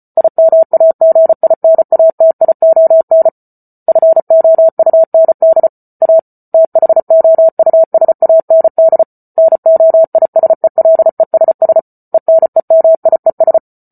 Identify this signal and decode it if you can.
Morse code, heard as a keyed tone, about 34 wpm, 650 Hz. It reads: IMAGINATION FOUND A THOUSAND NOISELESS ENEMIES